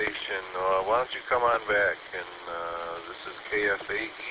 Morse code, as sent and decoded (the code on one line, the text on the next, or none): none